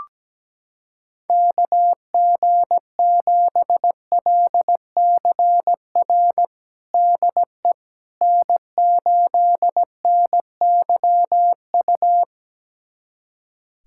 Morse code, heard as a keyed tone, about 17 wpm, 700 Hz. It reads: KG7LCR DE N8NYU
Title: USA Callsign DE Callsign 17wpm